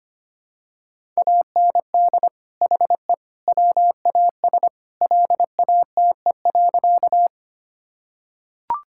Morse code, heard as a keyed tone, about 25 wpm, 700 Hz. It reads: AND HE WAS LATE.
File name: practice / 25wpm / Sentences with 3-Word Phrases - 25wpm